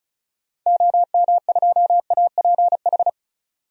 Morse code, heard as a keyed tone, about 36 wpm, 700 Hz. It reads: OM2APH
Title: DXCC-100 Call Signs - 6 Characters - 35wpm